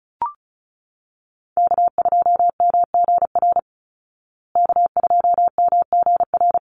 Morse code, heard as a keyed tone, about 34 wpm, 700 Hz. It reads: K2MGR K2MGR
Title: Call Signs Repeated 3x - 35wpm